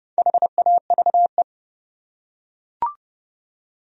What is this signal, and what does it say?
Morse code, heard as a keyed tone, about 28 wpm, 700 Hz. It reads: HAVE